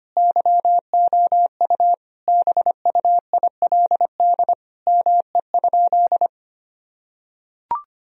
Morse code, heard as a keyed tone, about 25 wpm, 700 Hz. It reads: YOU BUILD ME?
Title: Sentences from Top 1000 Words - Review Part 1 - 25wpm